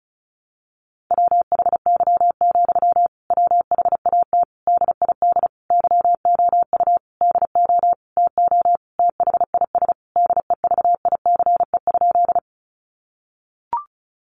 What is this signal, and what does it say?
Morse code, heard as a keyed tone, about 34 wpm, 700 Hz. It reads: WHY, WHAT DID YOU DO TO THIS DEVICE?
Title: Sentences from Top 1500 Words 35wpm